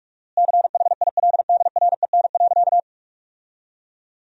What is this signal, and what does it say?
Morse code, heard as a keyed tone, about 46 wpm, 700 Hz. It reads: CHILDREN.
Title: 4x Word-Spacing - Sentences from Top 400 Words - 45wpm